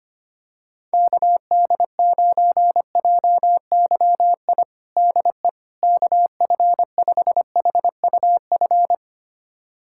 Morse code, heard as a keyed tone, about 25 wpm, 700 Hz. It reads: KD9JYI DE KF5HUF
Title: USA Callsign DE Callsign 25wpm